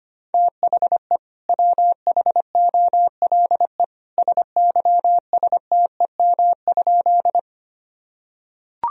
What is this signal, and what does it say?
Morse code, heard as a keyed tone, about 25 wpm, 700 Hz. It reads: THE WHOLE SYSTEM?
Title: Sentences from Top 1000 Words - Review Part 3 - 25wpm